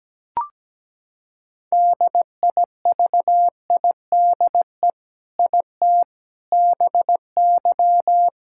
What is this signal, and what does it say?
Morse code, heard as a keyed tone, about 17 wpm, 700 Hz. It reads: DIVIDE IT BY
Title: Sentences from Top 1600 Words 17wpm